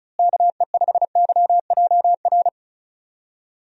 Morse code, heard as a keyed tone, about 34 wpm, 700 Hz. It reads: KE5YJR
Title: USA Call Signs - 6 Characters - Repeated 3x - 35wpm